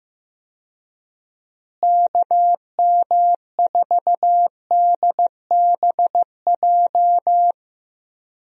Morse code, heard as a keyed tone, about 15 wpm, 700 Hz. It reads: KM4DBJ
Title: USA Call Signs - 6 Characters - Repeated 2x - 15wpm